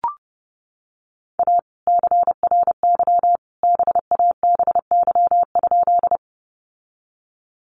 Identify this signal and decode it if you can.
Morse code, heard as a keyed tone, about 28 wpm, 700 Hz. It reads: A CRY BABY?
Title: Sentences from Top 1300 Words 30wpm